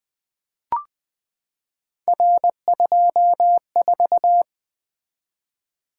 Morse code, heard as a keyed tone, about 20 wpm, 700 Hz. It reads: R24
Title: Three Letter-Number Combinations - 20wpm